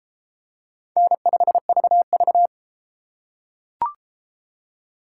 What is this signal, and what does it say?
Morse code, heard as a keyed tone, about 30 wpm, 700 Hz. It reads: N5VV